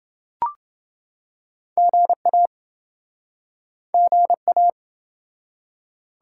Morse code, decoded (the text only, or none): GA GA